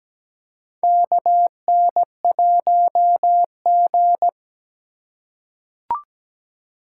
Morse code, heard as a keyed tone, about 17 wpm, 700 Hz. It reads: KN1G
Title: USA Call Signs - 2x Character Spacing - 17wpm